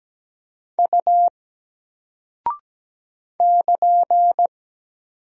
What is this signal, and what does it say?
Morse code, heard as a keyed tone, about 17 wpm, 700 Hz. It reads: U <KN>
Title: Single Character (Letter:Number:Punctuation:Prosign) 17wpm